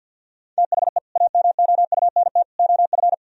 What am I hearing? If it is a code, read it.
Morse code, heard as a keyed tone, about 64 wpm, 700 Hz. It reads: THE AMOUNT OF